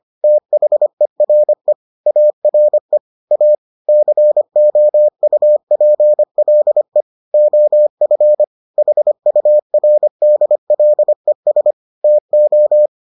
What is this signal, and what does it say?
Morse code, heard as a keyed tone, about 25 wpm, 600 Hz. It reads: THERE ARE A COUPLE OF HURDLES TO